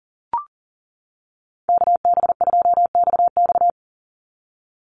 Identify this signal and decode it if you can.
Morse code, heard as a keyed tone, about 41 wpm, 700 Hz. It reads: KB2XX
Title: NA SOTA Chaser Callsigns 40wpm